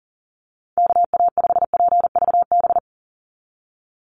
Morse code, heard as a keyed tone, about 40 wpm, 700 Hz. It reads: KA5PVB
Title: NA SOTA Chaser Callsigns 40wpm